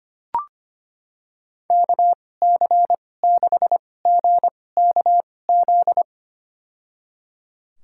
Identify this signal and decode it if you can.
Morse code, heard as a keyed tone, about 25 wpm, spaced as for 18 wpm, 700 Hz. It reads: KC6GKZ
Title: USA Call Signs - 6 Characters - 2x Character Spacing - 25wpm